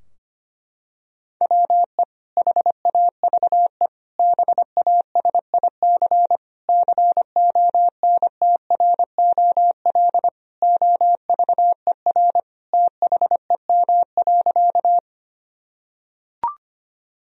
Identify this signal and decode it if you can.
Morse code, heard as a keyed tone, about 25 wpm, 700 Hz. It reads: WE HAVE BASIC CONTROL OVER THEM.
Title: Sentences from Top 1000 Words - Review Part 3 - 25wpm